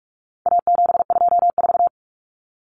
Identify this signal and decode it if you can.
Morse code, heard as a keyed tone, about 44 wpm, 700 Hz. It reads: A724
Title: Common Cut Numbers - 4-Digit 45wpm